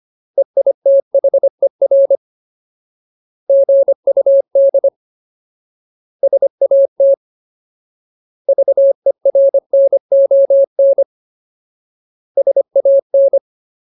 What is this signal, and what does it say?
Morse code, heard as a keyed tone, about 25 wpm, 550 Hz. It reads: EITHER GUD SAT VERNON SAN
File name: Ultimate Word List / 25WPM Ultimate